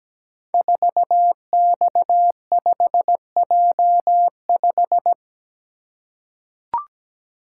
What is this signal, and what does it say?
Morse code, heard as a keyed tone, about 17 wpm, 700 Hz. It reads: 4X5J5
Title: Code Groups - 5 Letter:Numbers - 17wpm